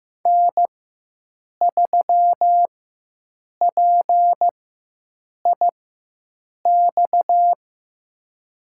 Morse code, heard as a keyed tone, about 15 wpm, spaced as for 7 wpm, 700 Hz. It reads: N3PIX
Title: USA Call Signs - 5 Characters - 4x Character Spacing - 15wpm